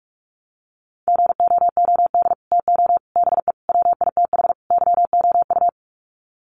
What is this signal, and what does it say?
Morse code, heard as a keyed tone, about 44 wpm, 700 Hz. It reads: GOOD TO BE WITH YOU
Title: Sets of 5 Words Encore2 45wpm